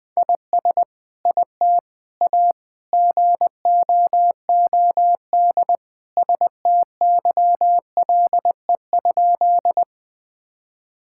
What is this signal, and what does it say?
Morse code, heard as a keyed tone, about 20 wpm, 700 Hz. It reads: IS IT A GOOD STYLE?
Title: Sentences from Top 1000 Words - Review Part 5 - 20wpm